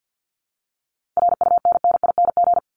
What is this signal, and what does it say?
Morse code, heard as a keyed tone, about 50 wpm, 700 Hz. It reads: RUNNING
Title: ING Suffix - Top 100 Words with Base Words 50wpm